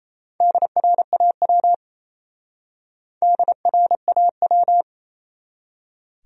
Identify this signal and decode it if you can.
Morse code, heard as a keyed tone, about 31 wpm, 700 Hz. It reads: DRAW DRAW
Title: Speed-Racing Top 300 Words 28wpm